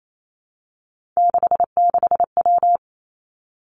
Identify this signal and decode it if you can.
Morse code, heard as a keyed tone, about 28 wpm, 700 Hz. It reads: 66W
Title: Three Challenging Letter-Number Combinations 28wpm